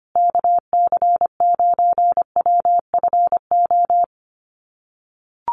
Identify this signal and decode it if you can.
Morse code, heard as a keyed tone, about 25 wpm, 700 Hz. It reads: KC9WFO
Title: USA Call Signs Repeated 3x 25wpm